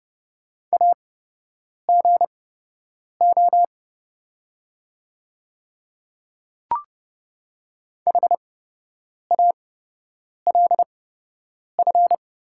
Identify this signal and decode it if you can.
Morse code, heard as a keyed tone, about 30 wpm, spaced as for 8 wpm, 700 Hz. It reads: AGO HALF
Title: 8x Character Spacing - Top 500 Words - 30wpm